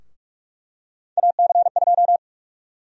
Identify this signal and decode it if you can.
Morse code, heard as a keyed tone, about 46 wpm, 700 Hz. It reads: AC2
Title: USA Call Sign Prefixes - 45wpm